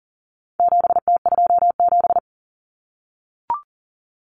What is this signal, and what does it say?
Morse code, heard as a keyed tone, about 39 wpm, 700 Hz. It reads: 7T27
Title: Common Cut Numbers - 4-Digit 40wpm